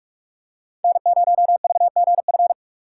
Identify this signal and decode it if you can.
Morse code, heard as a keyed tone, about 48 wpm, 700 Hz. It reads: N0VGF
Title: USA Call Signs - 6x Character Spacing - 45wpm